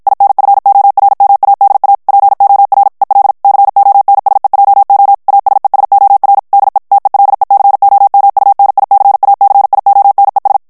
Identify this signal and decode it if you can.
Morse code, heard as a keyed tone, about 53 wpm, 800 Hz. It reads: INFORMADA POR EL CONSEJO ASESOR DE TELECOMUNICACIONES
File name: morse mp3 / ESPAGNOL TEXTE EN CLAIR 2 VITESSE 50 MOTS MINUTES